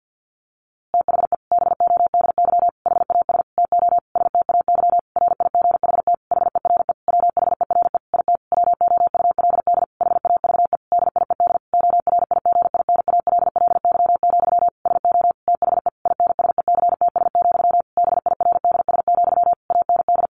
Morse code, heard as a keyed tone, about 50 wpm, 700 Hz. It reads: THE BODY HAS TO STAY RIGHT HERE WHERE IT WOULD HAVE DIED ORIGINALLY, SO THE INSECTS, BIRDS, AND
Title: A Sound Of Thunder 50wpm